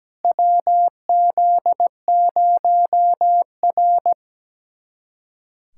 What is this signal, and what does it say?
Morse code, heard as a keyed tone, about 17 wpm, 700 Hz. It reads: WZ0R
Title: USA Callsign DE Callsign - 4 Characters - 17wpm